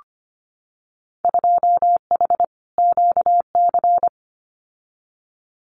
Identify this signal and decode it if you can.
Morse code, heard as a keyed tone, about 25 wpm, 700 Hz. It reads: 2H QC
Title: Winter Field Day Exchanges Encore 25wpm